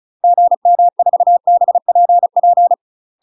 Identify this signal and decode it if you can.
Morse code, heard as a keyed tone, about 36 wpm, 700 Hz. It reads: GM4BPP
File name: UK MASTER QSOs TRAINING / 35WPM / UK MASTER QSOs 0032